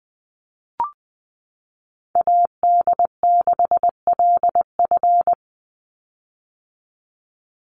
Morse code, heard as a keyed tone, about 20 wpm, 700 Hz. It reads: AD6LF
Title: USA Call Signs 20wpm